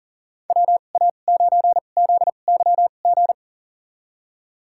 Morse code, heard as a keyed tone, about 40 wpm, spaced as for 29 wpm, 700 Hz. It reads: WA9ZYG
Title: USA Call Signs - 6 Characters - 2x Character Spacing - 40wpm